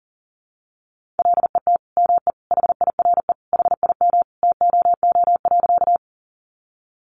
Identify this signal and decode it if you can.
Morse code, heard as a keyed tone, about 40 wpm, 700 Hz. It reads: LET ME HIRE HIM TOO.
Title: Sentences from Top 1400 Words 40wpm